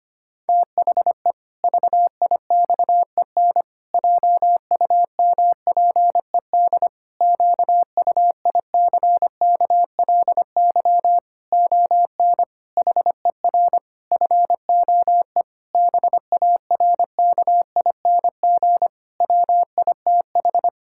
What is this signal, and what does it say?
Morse code, heard as a keyed tone, about 25 wpm, 700 Hz. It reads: THE VIXEN JUMPED QUICKLY ON HER FOE BARKING WITH
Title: Pangram Challenge v2 - 25wpm